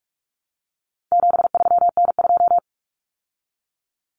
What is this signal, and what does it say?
Morse code, heard as a keyed tone, about 44 wpm, 700 Hz. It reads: 73N2
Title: Common Cut Numbers - 4-Digit 45wpm